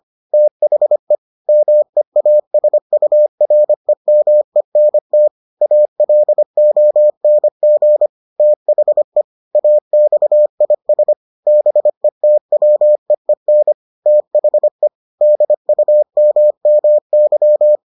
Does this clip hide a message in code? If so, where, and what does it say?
Yes: Morse code from 0.3 to 17.8 s, heard as a keyed tone, about 25 wpm, 600 Hz. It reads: THE MEASUREMENT ALONG THE AXIS BETWEEN THE DUMMY